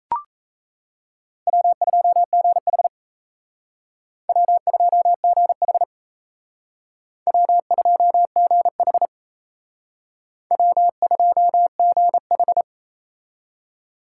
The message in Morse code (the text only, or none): W2GH W2GH W2GH W2GH